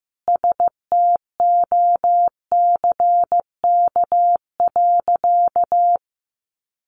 Morse code, heard as a keyed tone, about 15 wpm, 700 Hz. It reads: STOCK.
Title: Sentences from Top 1300 Words 15wpm